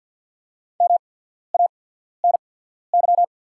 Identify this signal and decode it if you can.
Morse code, heard as a keyed tone, about 48 wpm, spaced as for 14 wpm, 700 Hz. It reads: MANY
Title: 8x Character Spacing - Top 100 Words - 50wpm